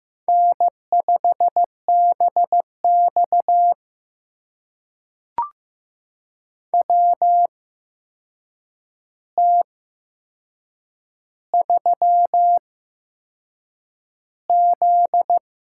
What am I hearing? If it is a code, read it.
Morse code, heard as a keyed tone, about 15 wpm, spaced as for 6 wpm, 700 Hz. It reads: N5BX WT3Z
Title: USA Call Signs - 4 Characters - 8x Character Spacing - 15wpm